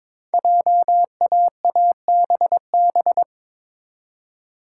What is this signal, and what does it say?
Morse code, heard as a keyed tone, about 23 wpm, 700 Hz. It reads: JAABB
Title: Code Groups - 5 Letters - 22wpm